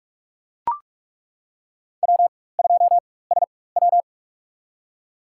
Morse code, heard as a keyed tone, about 41 wpm, spaced as for 20 wpm, 700 Hz. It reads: W2SW
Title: USA Call Signs - 4 Characters - 4x Character Spacing - 45wpm